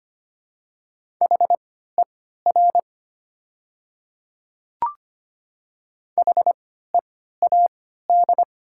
Morse code, heard as a keyed tone, about 25 wpm, spaced as for 14 wpm, 700 Hz. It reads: HER HEAD
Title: 3x Character Spacing - Top 300 Words - 25wpm